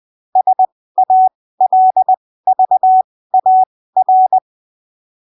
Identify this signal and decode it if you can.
Morse code, heard as a keyed tone, about 20 wpm, spaced as for 15 wpm, 750 Hz. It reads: SALVAR
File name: arquivos / texto / texto 03 20ppm